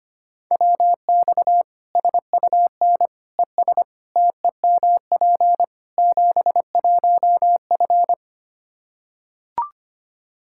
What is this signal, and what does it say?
Morse code, heard as a keyed tone, about 25 wpm, 700 Hz. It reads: WX SUN ES TEMP 71F
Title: Practice QSO - Exchange 2 - Info - 25wpm